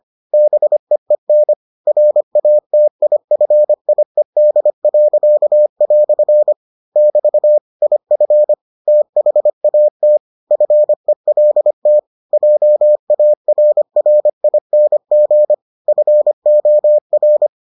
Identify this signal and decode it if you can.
Morse code, heard as a keyed tone, about 25 wpm, 600 Hz. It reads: BEEN RATIFIED." <BT> IF THAT FELT JARRING FOR